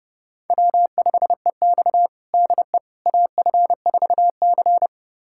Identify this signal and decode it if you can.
Morse code, heard as a keyed tone, about 32 wpm, 700 Hz. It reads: W5EX DE AF4C
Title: USA Callsign DE Callsign - 4 Characters - 30wpm